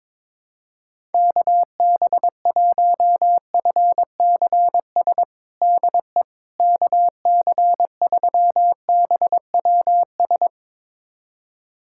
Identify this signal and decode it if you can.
Morse code, heard as a keyed tone, about 22 wpm, 700 Hz. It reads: KB1FCS DE KC3BWS
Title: USA Callsign DE Callsign 22wpm